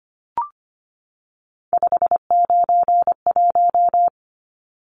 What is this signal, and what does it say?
Morse code, heard as a keyed tone, about 25 wpm, 700 Hz. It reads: 591
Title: Numbers - 3-Digit 25wpm